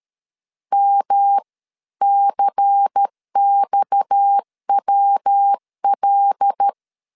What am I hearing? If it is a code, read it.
Morse code, heard as a keyed tone, about 13 wpm, spaced as for 11 wpm, 800 Hz. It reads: MCXWL
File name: CW / DET / 1 9 1500C S A 12wpm 3800